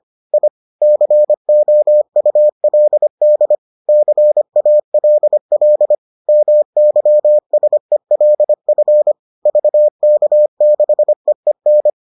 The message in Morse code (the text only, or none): I COULD CALL MYSELF VK6EEN